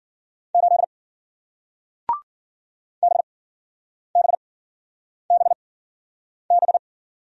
Morse code, heard as a keyed tone, about 54 wpm, spaced as for 9 wpm, 700 Hz. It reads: 8 BBBB